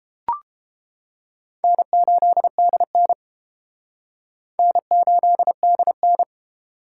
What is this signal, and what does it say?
Morse code, heard as a keyed tone, about 30 wpm, 700 Hz. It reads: N8DN N8DN